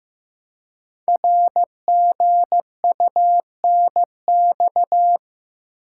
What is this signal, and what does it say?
Morse code, heard as a keyed tone, about 15 wpm, 700 Hz. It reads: RGUNX
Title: Code Groups - 5 Letters - 15wpm